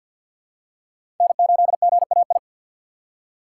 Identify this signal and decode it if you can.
Morse code, heard as a keyed tone, about 53 wpm, 700 Hz. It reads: N8GAI
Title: USA Call Signs - 5 Characters - 3x Character Spacing - 50wpm